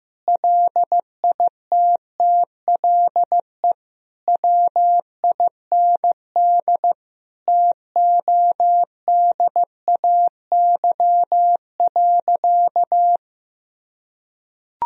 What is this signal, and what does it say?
Morse code, heard as a keyed tone, about 15 wpm, 700 Hz. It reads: LITTLE WIND TODAY.
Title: Sentences from Top 1000 Words - Review Part 2 - 15wpm